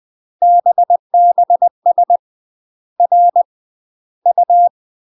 Morse code, heard as a keyed tone, about 20 wpm, 700 Hz. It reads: BBS R U
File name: UK MASTER QSOs TRAINING/20WPM